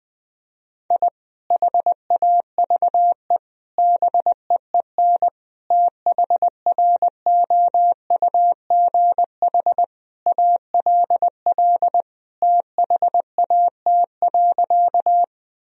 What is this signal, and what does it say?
Morse code, heard as a keyed tone, about 20 wpm, 700 Hz. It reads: I HAVE BEEN THROUGH ALL THAT.
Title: Sentences from Top 1000 Words - Review Part 1 - 20wpm